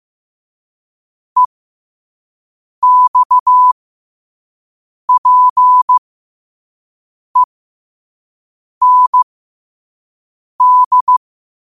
Morse code, heard as a keyed tone, about 15 wpm, spaced as for 5 wpm, 1000 Hz. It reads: EXPEND